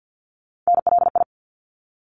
Morse code, heard as a keyed tone, about 50 wpm, 700 Hz. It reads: NLI
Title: ARRL Sections 50wpm